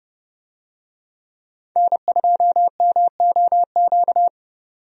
Morse code, heard as a keyed tone, about 30 wpm, 700 Hz. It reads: N2MOQ
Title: USA Call Signs - 5 Characters - 30wpm